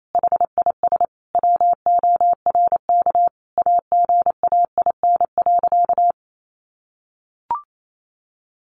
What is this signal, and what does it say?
Morse code, heard as a keyed tone, about 28 wpm, 700 Hz. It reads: HIS WORK AGAIN.
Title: Sentences from Top 1500 Words 28wpm